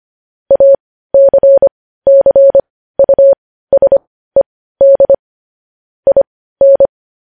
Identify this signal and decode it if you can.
Morse code, heard as a keyed tone, about 25 wpm, spaced as for 15 wpm, 550 Hz. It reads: ACCUSED IN
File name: podcast / News headlines-1x-25x15